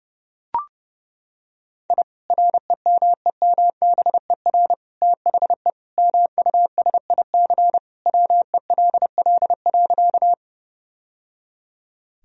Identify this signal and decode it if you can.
Morse code, heard as a keyed tone, about 32 wpm, 700 Hz. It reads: I REMEMBER THE MUSIC WELL.
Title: Sentences from Top 1000 Words - Review Part 1 - 30wpm